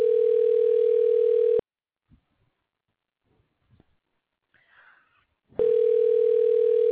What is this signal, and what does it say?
Ringback tone (North America), 440 + 480 Hz.